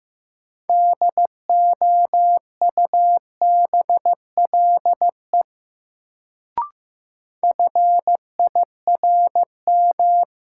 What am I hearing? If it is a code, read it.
Morse code, heard as a keyed tone, about 15 wpm, 700 Hz. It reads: DOUBLE FIRM
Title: Sentences from Top 1400 Words - New Word Drill Alt - 15wpm